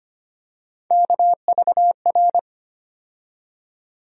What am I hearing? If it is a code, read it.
Morse code, heard as a keyed tone, about 25 wpm, 700 Hz. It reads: KVR
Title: USA Call Sign Postfixes - 3 Letters 25wpm